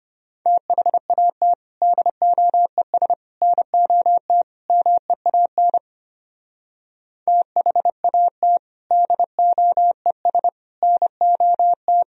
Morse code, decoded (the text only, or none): THAT DOES NOT MEAN THAT DOES NOT